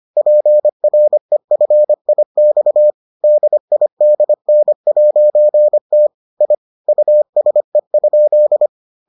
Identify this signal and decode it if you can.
Morse code, heard as a keyed tone, about 25 wpm, 600 Hz. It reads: PREFIX DIDN'T I USE?